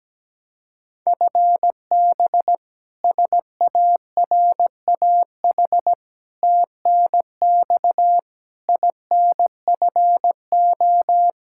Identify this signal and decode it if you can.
Morse code, heard as a keyed tone, about 17 wpm, 700 Hz. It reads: FB SARAH TNX INFO
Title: K1USN SST QSO Practice 17wpm